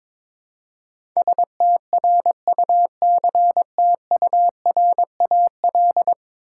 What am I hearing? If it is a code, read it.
Morse code, heard as a keyed tone, about 22 wpm, 700 Hz. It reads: STRUCTURAL
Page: morse-code-ninja-practice-22wpm